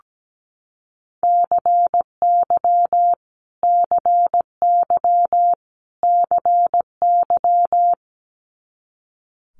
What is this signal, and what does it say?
Morse code, heard as a keyed tone, about 17 wpm, 700 Hz. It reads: CY CY CY
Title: Multi-letter Phonemes Repeated 3x 17wpm